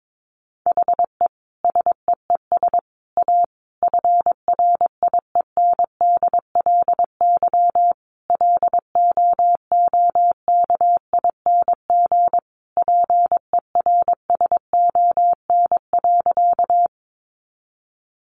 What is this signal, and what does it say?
Morse code, heard as a keyed tone, about 22 wpm, 700 Hz. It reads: HE SEES A FRIENDLY LOOKING PERSON.
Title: Easy Phrases 22wpm